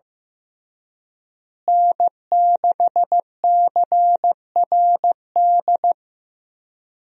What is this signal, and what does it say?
Morse code, heard as a keyed tone, about 15 wpm, 700 Hz. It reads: N6CRD
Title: USA Call Signs - 5 Characters - Repeated 3x - 15wpm